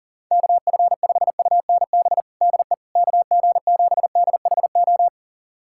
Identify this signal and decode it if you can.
Morse code, heard as a keyed tone, about 40 wpm, 700 Hz. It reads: KF5UNB DE KG7DHO
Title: USA Callsign DE Callsign - 6 Characters - 40wpm